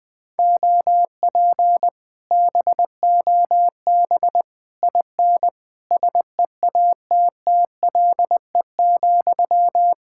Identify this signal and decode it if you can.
Morse code, heard as a keyed tone, about 19 wpm, 700 Hz. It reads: OP BOB IN SEATTLE,
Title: Speed-Racing Common QSO Phrases - 20wpm